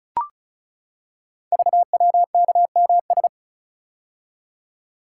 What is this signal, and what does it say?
Morse code, heard as a keyed tone, about 36 wpm, 700 Hz. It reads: VWKMS
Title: Code Groups - 5 Letters - 35wpm